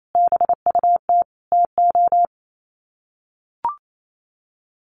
Morse code, heard as a keyed tone, about 28 wpm, 700 Hz. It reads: BUT TO
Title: Sets of 5 Words Encore 28wpm